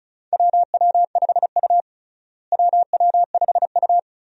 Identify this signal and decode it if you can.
Morse code, heard as a keyed tone, about 34 wpm, 700 Hz. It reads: WW5U WW5U
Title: USA Call Signs - 4 Characters - Repeated 2x - 35wpm